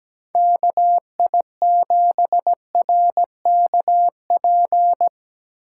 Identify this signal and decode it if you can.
Morse code, heard as a keyed tone, about 20 wpm, 700 Hz. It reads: KI7RKP
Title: Speed Racing - USA 2x3 Call Signs 17wpm